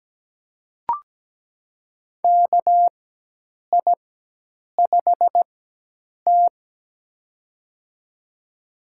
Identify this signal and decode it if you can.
Morse code, heard as a keyed tone, about 17 wpm, spaced as for 8 wpm, 700 Hz. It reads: KI5T
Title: USA Call Signs - 4x Character Spacing - 17wpm